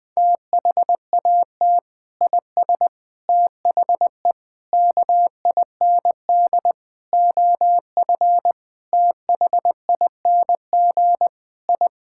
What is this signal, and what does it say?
Morse code, heard as a keyed tone, about 20 wpm, 700 Hz. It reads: THAT IS THE KIND OF THING I
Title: Sentences with 3-Word Phrases - 20wpm